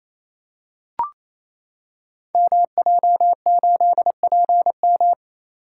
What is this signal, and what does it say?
Morse code, heard as a keyed tone, about 27 wpm, 700 Hz. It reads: MJ8PM
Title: Code Groups - 5 Letter:Numbers - 28wpm